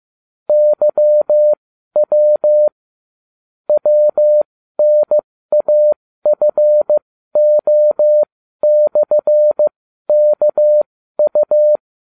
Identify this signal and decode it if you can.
Morse code, heard as a keyed tone, about 15 wpm, spaced as for 12 wpm, 600 Hz. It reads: YW WNAFO/KU